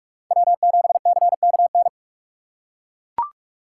Morse code, heard as a keyed tone, about 42 wpm, 700 Hz. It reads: W7CKN
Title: USA Call Signs - 5 Characters - 6x Character Spacing - 45wpm